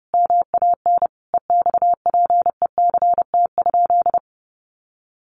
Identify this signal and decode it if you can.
Morse code, heard as a keyed tone, about 30 wpm, 700 Hz. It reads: MAN EXPECT?
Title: Sentences from Top 1100 Words 30wpm